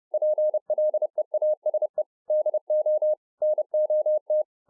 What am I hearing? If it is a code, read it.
Morse code, heard as a keyed tone, about 30 wpm, 610 Hz. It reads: PLEASE DO NOT